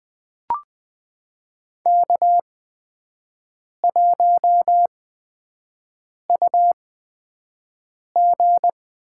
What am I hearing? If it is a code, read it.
Morse code, heard as a keyed tone, about 20 wpm, spaced as for 5 wpm, 700 Hz. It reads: K1UG
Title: USA Call Signs - 5 Characters - 8x Character Spacing - 20wpm